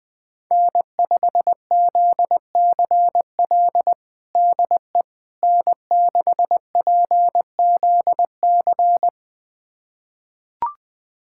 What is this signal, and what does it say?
Morse code, heard as a keyed tone, about 20 wpm, 700 Hz. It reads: N5ZCL DE N6PZC
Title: USA Callsign DE Callsign 20wpm